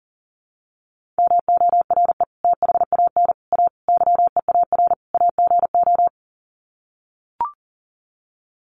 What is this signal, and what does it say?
Morse code, heard as a keyed tone, about 40 wpm, 700 Hz. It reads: MORE THAN A YEAR AGO
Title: Sets of 5 Words Encore2 40wpm